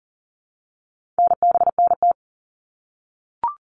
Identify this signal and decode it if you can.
Morse code, heard as a keyed tone, about 40 wpm, 700 Hz. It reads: NBNT
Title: All Cut Numbers - 4-Digit 40wpm